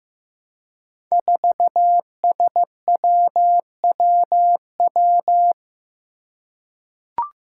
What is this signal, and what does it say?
Morse code, heard as a keyed tone, about 15 wpm, 700 Hz. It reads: 4SWWW
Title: Five Challenging Letter-Number Combinations 15wpm